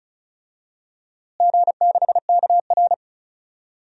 Morse code, heard as a keyed tone, about 36 wpm, 700 Hz. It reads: G6KR
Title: DXCC-100 Call Signs - 4 Characters - Repeated 2x - 35wpm